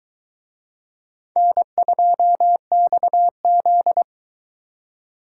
Morse code, heard as a keyed tone, about 22 wpm, 700 Hz. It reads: N2XZ